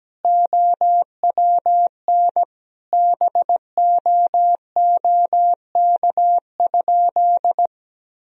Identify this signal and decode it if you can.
Morse code, heard as a keyed tone, about 17 wpm, 700 Hz. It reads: OWN BOOK?
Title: Sentences from Top 1000 Words - Review Part 3 - 17wpm